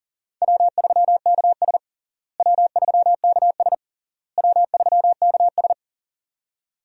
Morse code, heard as a keyed tone, about 37 wpm, 700 Hz. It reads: W3KS W3KS W3KS
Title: USA Call Signs - 4 Characters - Repeated 3x - 40wpm